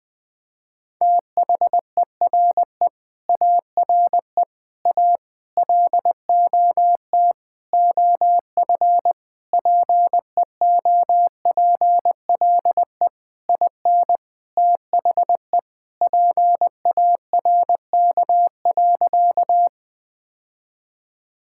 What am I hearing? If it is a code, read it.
Morse code, heard as a keyed tone, about 20 wpm, 700 Hz. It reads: THERE ARE A LOT OF PEOPLE IN THE PARK.
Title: Sentences with 5-Word Phrases - 20wpm